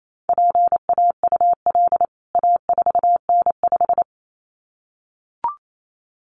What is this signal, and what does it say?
Morse code, heard as a keyed tone, about 28 wpm, 700 Hz. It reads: PAUL A4N5
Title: CWT Exchange 28wpm